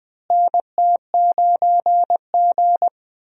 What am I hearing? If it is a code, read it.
Morse code, heard as a keyed tone, about 20 wpm, 700 Hz. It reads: NT9G
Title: USA Call Signs - 4 Characters - Repeated 2x - 20wpm